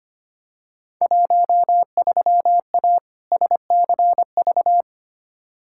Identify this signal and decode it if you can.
Morse code, heard as a keyed tone, about 25 wpm, 700 Hz. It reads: 13A SCV
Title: Field Day Exchanges Encore - 25wpm